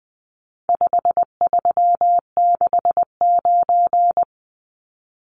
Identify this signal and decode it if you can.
Morse code, heard as a keyed tone, about 20 wpm, 700 Hz. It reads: 5369
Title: Numbers - 4-Digit 20wpm